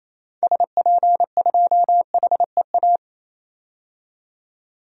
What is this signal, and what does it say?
Morse code, heard as a keyed tone, about 28 wpm, 700 Hz. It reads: SP2HEA
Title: DXCC-100 Call Signs - 6 Characters - 28wpm